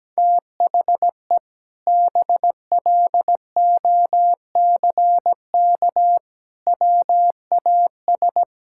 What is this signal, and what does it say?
Morse code, heard as a keyed tone, about 17 wpm, 700 Hz. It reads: THE BLOCK WAS
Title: Sentences from Top 1600 Words 17wpm